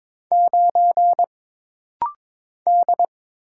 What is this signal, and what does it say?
Morse code, heard as a keyed tone, about 22 wpm, 700 Hz. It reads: 9 D
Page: This rapid-fire timing is geared to push your ICR (Instant Character Recognition) skills.